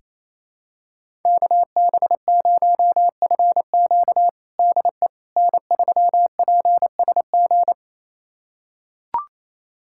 Morse code, heard as a keyed tone, about 28 wpm, 700 Hz. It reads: KB0FQ DE N3PSG
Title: USA Callsign DE Callsign - 5 Characters - 28wpm